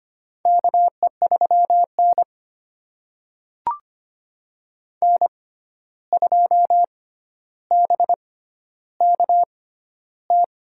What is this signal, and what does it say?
Morse code, heard as a keyed tone, about 25 wpm, spaced as for 12 wpm, 700 Hz. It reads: KE3N N2BKT
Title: USA Call Signs - 6x Character Spacing - 25wpm